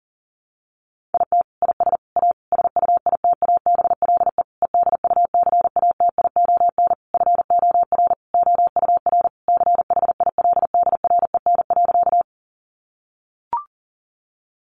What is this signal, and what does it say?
Morse code, heard as a keyed tone, about 40 wpm, 700 Hz. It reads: IT IS A SUITABLE EDUCATION FOR OUR CHILDREN.
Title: Sentences from Top 1400 Words 40wpm